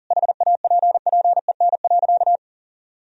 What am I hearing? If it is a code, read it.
Morse code, heard as a keyed tone, about 40 wpm, 700 Hz. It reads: HAPPEN.
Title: Sentences from Top 1000 Words - Review Part 3 - 40wpm